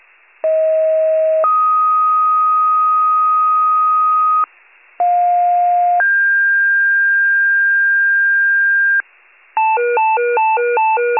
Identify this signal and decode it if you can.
Two-tone page, heard as a one-second tone, then a three-second tone, about 640 then 1250 Hz, about 710 then 1600 Hz, with noise. It is heard from 0.4 to 9.0 s.
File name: pager / 38 FIRE / wcalert